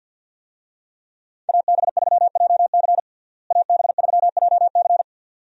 Morse code, heard as a keyed tone, about 51 wpm, 700 Hz. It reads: AB3JC AB3JC
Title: USA Call Signs - 5 Characters - Repeated 2x - 50wpm